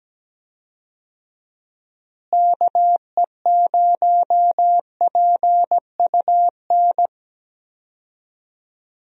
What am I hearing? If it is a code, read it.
Morse code, heard as a keyed tone, about 17 wpm, 700 Hz. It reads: KE0PUN
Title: USA Call Signs - 6 Characters - Repeated 2x - 17wpm